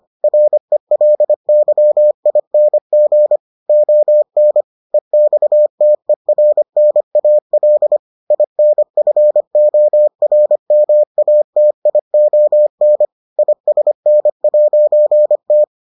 Morse code, heard as a keyed tone, about 25 wpm, 600 Hz. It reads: RELYING ON EXTERNAL INFORMATION ISN'T